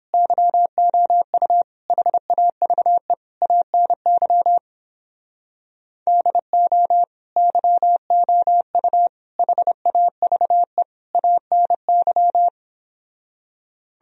Morse code, heard as a keyed tone, about 29 wpm, 700 Hz. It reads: YOU HAVE ANY DO YOU HAVE ANY